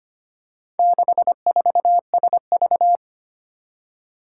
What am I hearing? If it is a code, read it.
Morse code, heard as a keyed tone, about 25 wpm, 700 Hz. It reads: 64SV
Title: Four Challenging Letter-Number Combinations 25wpm